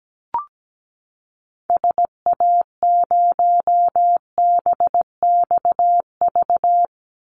Morse code, heard as a keyed tone, about 17 wpm, 700 Hz. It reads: SA0BXV
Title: Call Signs Repeated 3x - 17wpm